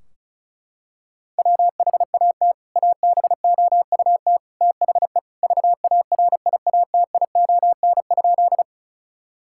Morse code, heard as a keyed tone, about 36 wpm, 700 Hz. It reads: WHAT ABOUT THE VARIATION?
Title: Sentences from Top 1600 Words 35wpm